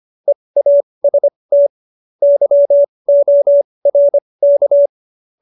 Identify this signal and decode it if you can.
Morse code, heard as a keyed tone, about 25 wpm, spaced as for 20 wpm, 570 Hz. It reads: EAST YORK
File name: qso220b 20